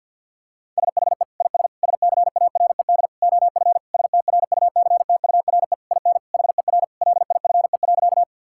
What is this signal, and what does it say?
Morse code, heard as a keyed tone, about 50 wpm, 700 Hz. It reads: SHE IS SCARED OF STRUCTURE IN HER LIFE.
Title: Sentences from Top 1000 Words - Review Part 6 - 50wpm